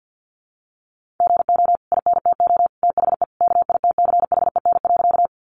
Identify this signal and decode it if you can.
Morse code, heard as a keyed tone, about 50 wpm, 700 Hz. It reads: GO INTO THE KITCHEN.
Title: Sentences from Top 1400 Words 50wpm